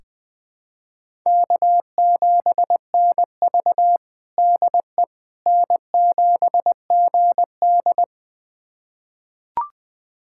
Morse code, heard as a keyed tone, about 20 wpm, 700 Hz. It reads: K7NV DE N7GD
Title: USA Callsign DE Callsign - 4 Characters - 20wpm